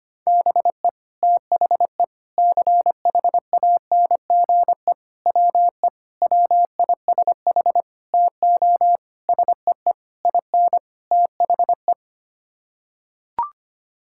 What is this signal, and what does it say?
Morse code, heard as a keyed tone, about 25 wpm, 700 Hz. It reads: BE THE CHANGE WE WISH TO SEE IN THE
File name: Sets of 9 Words - 25wpm